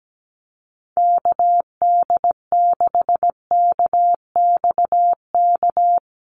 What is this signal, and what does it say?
Morse code, heard as a keyed tone, about 17 wpm, 700 Hz. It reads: KD6KXK
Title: USA Call Signs Repeated 2x 17wpm